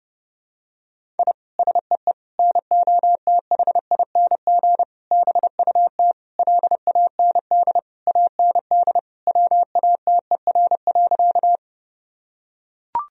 Morse code, heard as a keyed tone, about 32 wpm, 700 Hz. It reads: I SEE NOTHING BUT LAND AND WATER.
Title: Sentences from Top 1000 Words - Review Part 1 - 30wpm